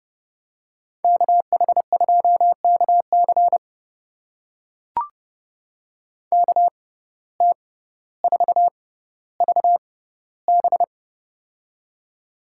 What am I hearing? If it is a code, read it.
Morse code, heard as a keyed tone, about 30 wpm, spaced as for 15 wpm, 700 Hz. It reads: KH2KC KT4VB
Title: USA Call Signs - 6x Character Spacing - 30wpm